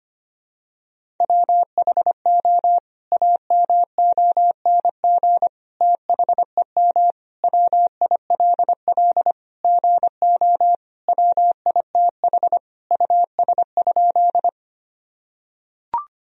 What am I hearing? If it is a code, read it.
Morse code, heard as a keyed tone, about 25 wpm, 700 Hz. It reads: WHO AMONG THEM WILL GO WITH US?
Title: Sentences from Top 1000 Words - Review Part 3 - 25wpm